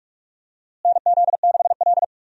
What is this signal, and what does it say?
Morse code, heard as a keyed tone, about 46 wpm, 700 Hz. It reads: NZ6L
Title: USA Call Signs - 4 Characters - Repeated 2x - 45wpm